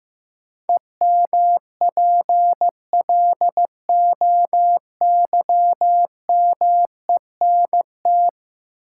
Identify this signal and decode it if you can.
Morse code, heard as a keyed tone, about 15 wpm, 700 Hz. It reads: EMPLOYMENT
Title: Sentences from Top 1500 Words - New Word Drill Alt - 15wpm